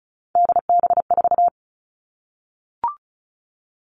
Morse code, heard as a keyed tone, about 36 wpm, 700 Hz. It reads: DB4
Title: Three Challenging Letter-Number Combinations 35wpm